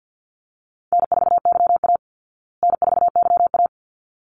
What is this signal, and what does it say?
Morse code, heard as a keyed tone, about 50 wpm, 700 Hz. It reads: N4YA N4YA
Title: Easy Call Signs Repeated 2x - 50wpm